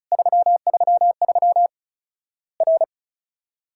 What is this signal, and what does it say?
Morse code, heard as a keyed tone, about 34 wpm, 700 Hz. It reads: 333 R
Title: Worked All Europe DX Contest - QTC Exchange - 35wpm